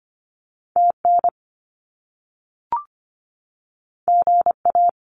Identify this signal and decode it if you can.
Morse code, heard as a keyed tone, about 25 wpm, 700 Hz. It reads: TN GA
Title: US States 25wpm